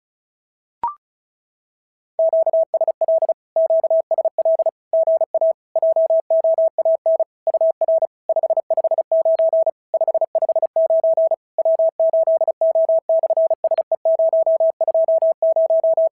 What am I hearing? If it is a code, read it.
Morse code, heard as a keyed tone, about 36 wpm, 650 Hz. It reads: QSL QSL GA JOAN UR 559 559 W8O/SE020
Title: SOTA QSO Practice 35wpm